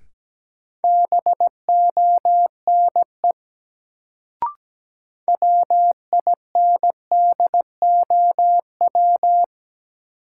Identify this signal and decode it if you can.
Morse code, heard as a keyed tone, about 17 wpm, 700 Hz. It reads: BONE WINDOW
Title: Sentences from Top 1600 Words - New Word Drill Alt - 17wpm